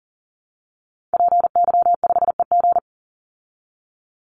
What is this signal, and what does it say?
Morse code, heard as a keyed tone, about 40 wpm, 700 Hz. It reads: PY5EG
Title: Call Signs Repeated 2x - 40wpm